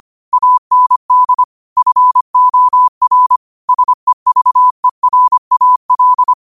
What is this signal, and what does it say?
Morse code, heard as a keyed tone, about 25 wpm, 1000 Hz. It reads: AND FOR SEVERAL
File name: qotdfiles / 25wpm